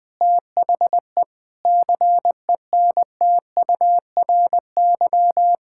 Morse code, heard as a keyed tone, about 21 wpm, 700 Hz. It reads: THE CENTURY
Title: Sentences with 4-Word Phrases - 20wpm